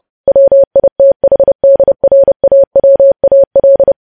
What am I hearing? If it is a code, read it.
Morse code, heard as a keyed tone, about 30 wpm, 550 Hz. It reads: WITHDRAWAL
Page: news headlines in morse code at 30 wpm